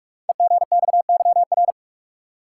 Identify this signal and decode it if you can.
Morse code, heard as a keyed tone, about 44 wpm, 700 Hz. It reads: EGXYR
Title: Code Groups - 5 Letters - 45wpm